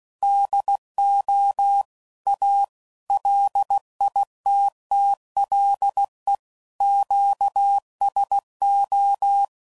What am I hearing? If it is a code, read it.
Morse code, heard as a keyed tone, about 16 wpm, 800 Hz. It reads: DO A LITTLE QSO
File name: CW Code Groups @15WPM / QSOs / WD0FBT